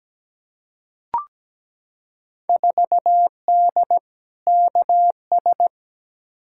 Morse code, heard as a keyed tone, about 17 wpm, 700 Hz. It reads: 4D KS
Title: Field Day Exchanges Encore - 17wpm